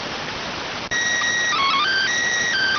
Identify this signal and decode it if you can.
SSTV (slow-scan television), PD 290.